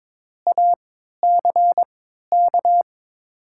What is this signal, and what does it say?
Morse code, heard as a keyed tone, about 22 wpm, spaced as for 12 wpm, 700 Hz. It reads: ACK
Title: 3x Character Spacing - Top 300 Words - 22wpm